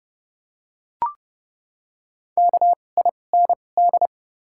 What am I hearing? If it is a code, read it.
Morse code, heard as a keyed tone, about 30 wpm, spaced as for 22 wpm, 700 Hz. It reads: KIND
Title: 2x Character Spacing - Top 200 Words - 30wpm